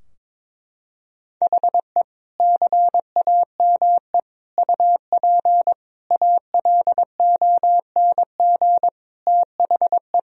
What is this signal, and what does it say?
Morse code, heard as a keyed tone, about 21 wpm, 700 Hz. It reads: HE CAME UP ALONG THE
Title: Sentences from Top 1000 Words - Review Part 4 - 22wpm